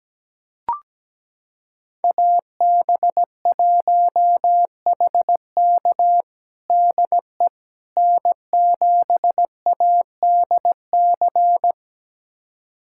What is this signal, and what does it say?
Morse code, heard as a keyed tone, about 17 wpm, 700 Hz. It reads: AB1HK DE N7ADC
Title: USA Callsign DE Callsign - 5 Characters - 17wpm